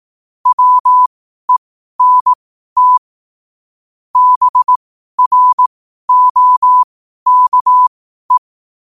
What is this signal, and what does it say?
Morse code, heard as a keyed tone, about 18 wpm, spaced as for 12 wpm, 1000 Hz. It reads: WENT BROKE
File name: qotdfiles / 12wpm